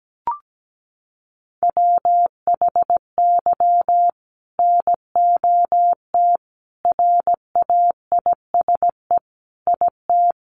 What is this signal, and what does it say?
Morse code, heard as a keyed tone, about 17 wpm, 700 Hz. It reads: WHY NOT RAISE IT
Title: Sentences from Top 1400 Words 17wpm